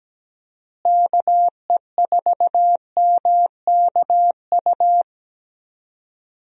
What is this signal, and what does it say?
Morse code, heard as a keyed tone, about 17 wpm, 700 Hz. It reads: KE4MKU
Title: USA Call Signs - 6 Characters - Repeated 3x - 17wpm